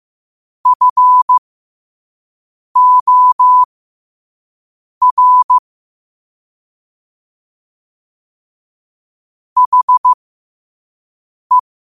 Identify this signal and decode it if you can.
Morse code, heard as a keyed tone, about 15 wpm, spaced as for 5 wpm, 1000 Hz. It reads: FOR HE
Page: Quotes for Sat, 16 Aug 2025 in Morse Code at 5 words per minute.